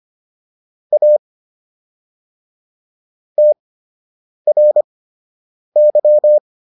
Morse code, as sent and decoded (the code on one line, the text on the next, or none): .- / - .-. -.--
A TRY